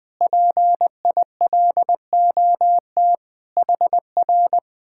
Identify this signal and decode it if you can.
Morse code, heard as a keyed tone, about 20 wpm, 700 Hz. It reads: PILOT HR
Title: Speed-Racing Common QSO Phrases Encore - 17wpm